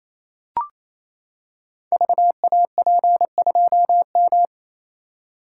Morse code, heard as a keyed tone, about 27 wpm, 700 Hz. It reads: VAP2M
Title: Code Groups - 5 Letter:Numbers - 28wpm